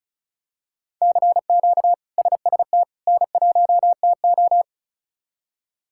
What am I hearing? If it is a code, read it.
Morse code, heard as a keyed tone, about 36 wpm, 700 Hz. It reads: CQ SST N1TO
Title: K1USN SST QSO Practice - 35wpm